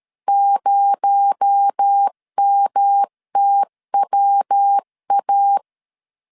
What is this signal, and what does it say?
Morse code, heard as a keyed tone, about 13 wpm, 800 Hz. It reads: 0MTWA
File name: CW / DET / 1 6 1500C S C 12wpm 3533